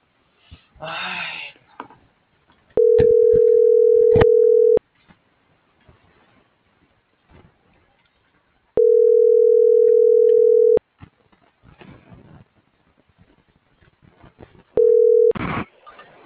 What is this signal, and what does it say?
Ringback tone (North America), 440 + 480 Hz.